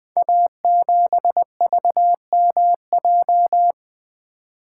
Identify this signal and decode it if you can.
Morse code, heard as a keyed tone, about 20 wpm, 700 Hz. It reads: A7VMJ
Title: Code Groups - 5 Letter:Numbers - 20wpm